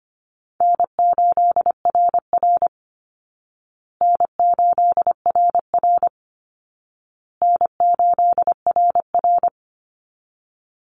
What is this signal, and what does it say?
Morse code, heard as a keyed tone, about 25 wpm, 700 Hz. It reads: N8RR N8RR N8RR
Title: Easy Call Signs Repeated 3x - 25wpm